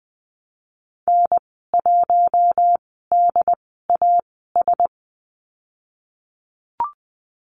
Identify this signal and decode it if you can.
Morse code, heard as a keyed tone, about 20 wpm, spaced as for 15 wpm, 700 Hz. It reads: N1DAS
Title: 088 Lesson 15g - Number 5 - Call Signs 20wpm